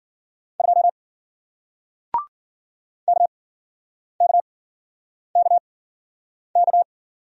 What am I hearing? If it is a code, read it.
Morse code, heard as a keyed tone, about 48 wpm, spaced as for 9 wpm, 700 Hz. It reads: 2 KKKK
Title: Speed-Racing Single Letter-Number 40wpm